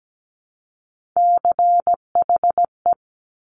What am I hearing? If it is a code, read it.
Morse code, heard as a keyed tone, about 17 wpm, 700 Hz. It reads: CHE
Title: Top Three Letter Combinations in Words 17wpm